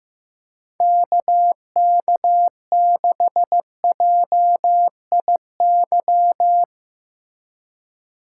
Morse code, heard as a keyed tone, about 17 wpm, 700 Hz. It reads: KK6JIY
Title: Speed Racing - USA 2x3 Call Signs 15wpm